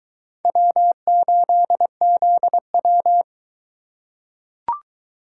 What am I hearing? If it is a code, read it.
Morse code, heard as a keyed tone, about 21 wpm, 700 Hz. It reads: W8ZW